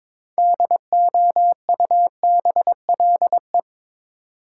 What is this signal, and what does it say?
Morse code, heard as a keyed tone, about 22 wpm, 700 Hz. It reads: DOUBLE
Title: Sentences from Top 1400 Words - New Word Drill Alt - 22wpm